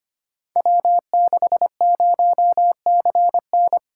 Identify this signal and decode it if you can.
Morse code, heard as a keyed tone, about 25 wpm, 700 Hz. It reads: W60CN
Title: Code Groups - 5 Letter:Numbers - 25wpm